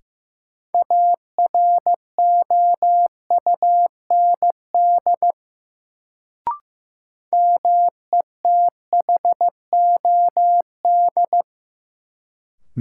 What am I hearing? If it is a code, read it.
Morse code, heard as a keyed tone, about 15 wpm, 700 Hz. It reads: AROUND METHOD
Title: Sentences from Top 600 Words - New Word Drill - 15wpm